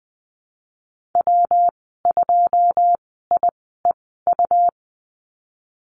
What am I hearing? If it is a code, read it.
Morse code, heard as a keyed tone, about 20 wpm, spaced as for 15 wpm, 700 Hz. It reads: W2IEU
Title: 154 Lesson 25g - Letter p - Call Signs 20wpm